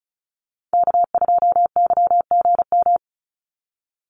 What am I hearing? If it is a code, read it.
Morse code, heard as a keyed tone, about 34 wpm, 700 Hz. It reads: K2YGM
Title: Call Signs Repeated 3x - 35wpm